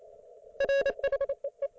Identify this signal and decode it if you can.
Morse code, heard as a keyed tone, about 28 wpm, 550 Hz. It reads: RHEE